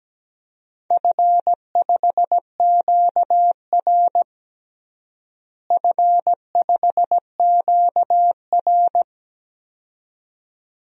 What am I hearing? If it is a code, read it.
Morse code, heard as a keyed tone, about 17 wpm, 700 Hz. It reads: F5QR F5QR
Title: DXCC-100 Call Signs - 4 Characters - Repeated 2x - 17wpm